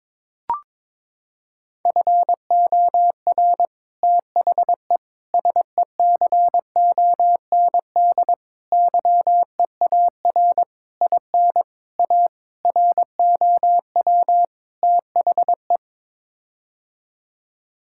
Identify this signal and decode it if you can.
Morse code, heard as a keyed tone, about 22 wpm, 700 Hz. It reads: FOR THE SECOND YEAR IN A ROW THE
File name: practice / 22wpm / Sets of 8 Words - 22wpm